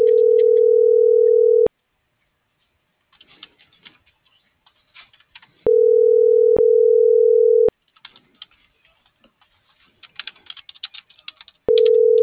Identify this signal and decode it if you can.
Ringback tone (North America), 440 + 480 Hz.